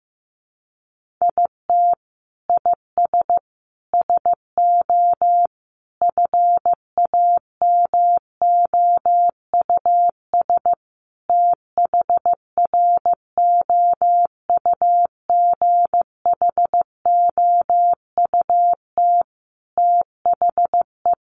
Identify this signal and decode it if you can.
Morse code, heard as a keyed tone, about 15 wpm, 700 Hz. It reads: IT IS SO FAMOUS THROUGHOUT THE
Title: Sentences from Top 1500 Words 15wpm